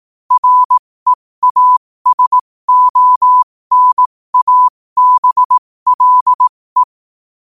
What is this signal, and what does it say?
Morse code, heard as a keyed tone, about 18 wpm, spaced as for 15 wpm, 1000 Hz. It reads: REASONABLE
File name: qotdfiles / 15wpm